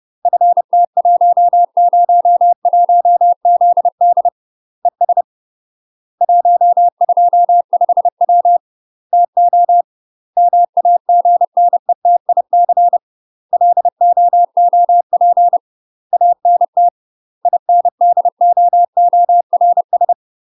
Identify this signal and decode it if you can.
Morse code, heard as a keyed tone, about 30 wpm, 700 Hz. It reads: FT101ZD ES 125W TO MAGNETIC LOOP ANT INDOORS
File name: UK MASTER QSOs TRAINING / 30WPM / UK MASTER QSOs 0027